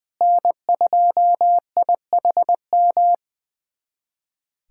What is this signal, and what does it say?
Morse code, heard as a keyed tone, about 20 wpm, 700 Hz. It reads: N2IHM
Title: USA Callsign DE Callsign - 5 Characters - 20wpm